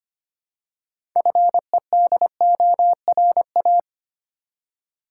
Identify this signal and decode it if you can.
Morse code, heard as a keyed tone, about 25 wpm, 700 Hz. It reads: FEDORA
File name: practice / 25wpm / Difficult Words with Hints - 25wpm